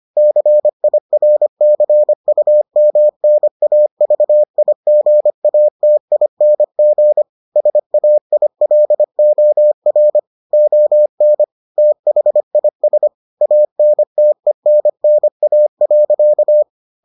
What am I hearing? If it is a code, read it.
Morse code, heard as a keyed tone, about 25 wpm, 600 Hz. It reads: CIRCUMNAVIGATING SAILOR ON THIS ANTENNA.